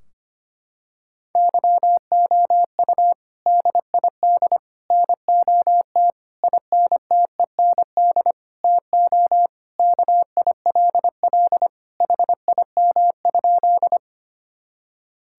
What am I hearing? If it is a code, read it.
Morse code, heard as a keyed tone, about 25 wpm, 700 Hz. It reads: YOU DID NOT INTEND TO KILL HIM?
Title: Sentences from Top 1600 Words 25wpm